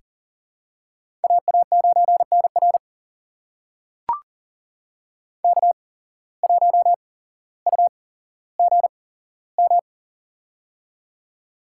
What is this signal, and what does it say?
Morse code, heard as a keyed tone, about 40 wpm, spaced as for 17 wpm, 700 Hz. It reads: AA9NR K1UGM
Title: USA Call Signs - 5 Characters - 8x Character Spacing - 40wpm